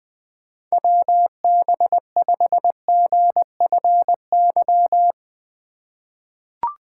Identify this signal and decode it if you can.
Morse code, heard as a keyed tone, about 20 wpm, 700 Hz. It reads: WB5GFY
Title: USA Call Signs - 6 Characters - 2x Character Spacing - 20wpm